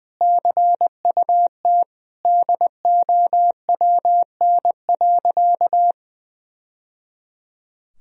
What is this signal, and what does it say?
Morse code, heard as a keyed tone, about 20 wpm, 700 Hz. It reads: CUT DOWN.
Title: Sentences from Top 1000 Words - Review Part 1 - 20wpm